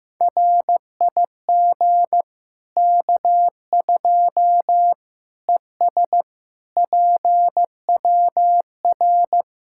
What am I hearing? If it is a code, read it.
Morse code, heard as a keyed tone, about 15 wpm, 700 Hz. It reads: RIG K2 ES PWR
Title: Practice QSO - Exchange 2 - Info - 15wpm